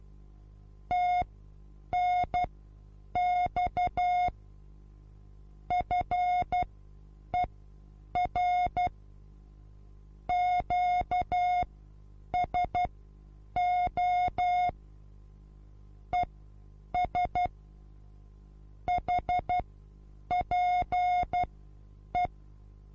Morse code, heard as a keyed tone, about 12 wpm, spaced as for 8 wpm, 720 Hz. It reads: TNX FER QSO ES HPE
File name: Onlinelehrgang / morsen / DJ4UF Morsen23